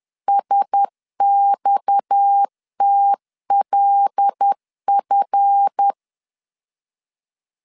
Morse code, heard as a keyed tone, about 10 wpm, 800 Hz. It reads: SXTLF
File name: LEZIONI / 7 GF9Z / 7 GF9Z 10wpm 2611